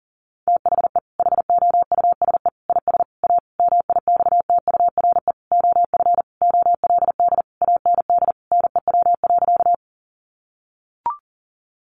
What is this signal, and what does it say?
Morse code, heard as a keyed tone, about 40 wpm, 700 Hz. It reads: THE HOUSE IS A MIXTURE OF OLD AND NEW.
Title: Sentences from Top 1500 Words 40wpm